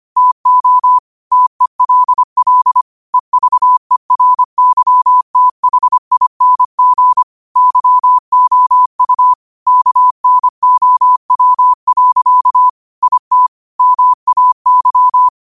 Morse code, heard as a keyed tone, about 25 wpm, 1000 Hz. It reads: TO TELL EVERYTHING YOU KNOW. IT MAY
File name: qotdfiles / 25wpm